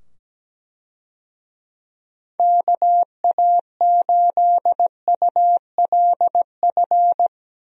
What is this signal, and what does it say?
Morse code, heard as a keyed tone, about 17 wpm, 700 Hz. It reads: KA8ULF
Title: USA Call Signs - 6 Characters - 17wpm